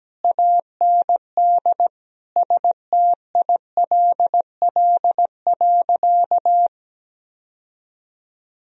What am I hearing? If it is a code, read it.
Morse code, heard as a keyed tone, about 17 wpm, 700 Hz. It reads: AND STILL.
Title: Sentences from Top 1000 Words - Review Part 2 - 17wpm